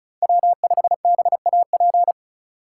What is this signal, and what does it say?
Morse code, heard as a keyed tone, about 34 wpm, 700 Hz. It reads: W5BAP
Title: USA Call Signs - 5 Characters - 2x Character Spacing - 35wpm